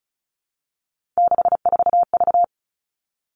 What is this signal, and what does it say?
Morse code, heard as a keyed tone, about 34 wpm, 700 Hz. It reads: 64V
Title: All Cut Numbers - 3-Digit 35wpm